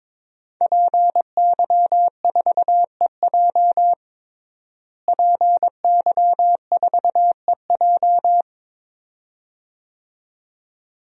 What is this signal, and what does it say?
Morse code, heard as a keyed tone, about 21 wpm, 700 Hz. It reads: PY4EJ PY4EJ
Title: DXCC-100 Call Signs - 5 Characters - Repeated 2x - 22wpm